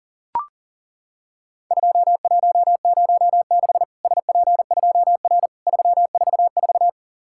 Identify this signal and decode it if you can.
Morse code, heard as a keyed tone, about 40 wpm, 700 Hz. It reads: 2106 SP2R 344
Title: Worked All Europe DX Contest - QTC Exchange - 40wpm